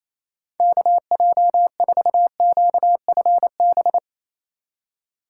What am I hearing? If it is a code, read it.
Morse code, heard as a keyed tone, about 27 wpm, 700 Hz. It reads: KJ4QFB
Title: USA Call Signs - 6 Characters - 3x Character Spacing - 28wpm